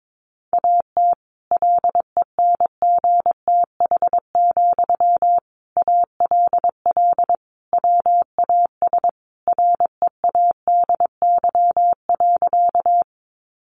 Morse code, heard as a keyed tone, about 22 wpm, 700 Hz. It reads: AT LENGTH, ALL WAS READY.
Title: Sentences from Top 1000 Words 22wpm